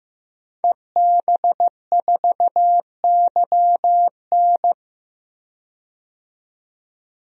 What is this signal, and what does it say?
Morse code, heard as a keyed tone, about 16 wpm, 700 Hz. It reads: EB4YN
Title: DXCC-100 Call Signs - 5 Characters - 15wpm